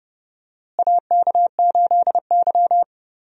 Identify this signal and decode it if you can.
Morse code, heard as a keyed tone, about 32 wpm, 700 Hz. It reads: AK8Y
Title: USA Call Signs - 4 Characters - Repeated 2x - 30wpm